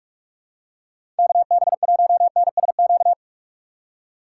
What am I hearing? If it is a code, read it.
Morse code, heard as a keyed tone, about 46 wpm, 700 Hz. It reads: KB1NSQ
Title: USA Call Signs - 6 Characters - 4x Character Spacing - 45wpm